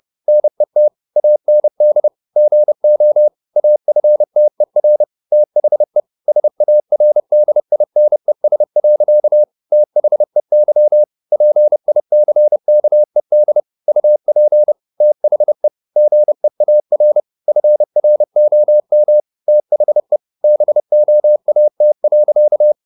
Morse code, heard as a keyed tone, about 30 wpm, 600 Hz. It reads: NET AND GO AFTER THE SARDINES. THEY PICKED UP THE GEAR FROM THE BOAT.
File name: the old man and the sea / the old man and the sea 1 30wpm